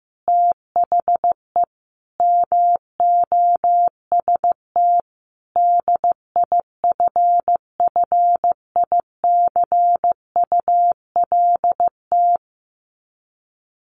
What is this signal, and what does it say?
Morse code, heard as a keyed tone, about 15 wpm, 700 Hz. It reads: THE MOST DIFFICULT
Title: Sets of 5 Words Encore2 15wpm